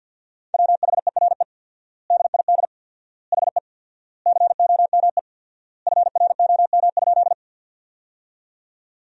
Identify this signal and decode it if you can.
Morse code, heard as a keyed tone, about 52 wpm, 700 Hz. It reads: WHERE DID HE COME FROM?
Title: 4x Word-Spacing - Sentences from Top 200 Words - 50wpm